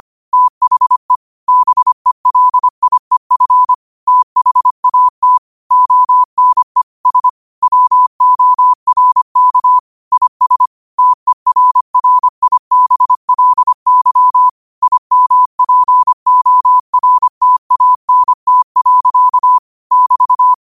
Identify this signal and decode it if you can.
Morse code, heard as a keyed tone, about 25 wpm, 1000 Hz. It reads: THE BELIEF THAT ONE S WORK IS TERRIBLY IMPORTANT. <BT>
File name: qotdfiles / 25wpm